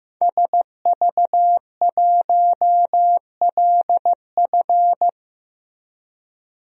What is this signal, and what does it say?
Morse code, heard as a keyed tone, about 15 wpm, 700 Hz. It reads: SV1LF
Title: DXCC-100 Call Signs - 5 Characters - Repeated 3x - 15wpm